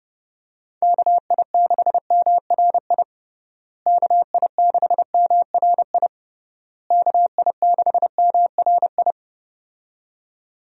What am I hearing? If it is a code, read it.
Morse code, heard as a keyed tone, about 30 wpm, 700 Hz. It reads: KI6MRI KI6MRI KI6MRI
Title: USA Call Signs - 6 Characters - Repeated 3x - 30wpm